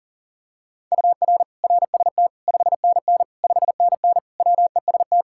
Morse code, heard as a keyed tone, about 37 wpm, 700 Hz. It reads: UR RST 5NN 5NN WEST
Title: Practice QSO - Exchange 1 - Beginning - 40wpm